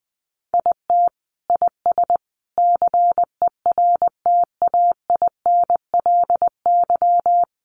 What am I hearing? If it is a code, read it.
Morse code, heard as a keyed tone, about 20 wpm, 700 Hz. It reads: IT IS CERTAINLY
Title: Sentences from Top 1100 Words 20wpm